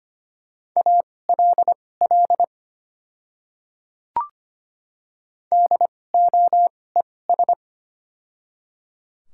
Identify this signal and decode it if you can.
Morse code, heard as a keyed tone, about 25 wpm, spaced as for 18 wpm, 700 Hz. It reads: ALL DOES
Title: 2x Character Spacing - Top 300 Words - 25wpm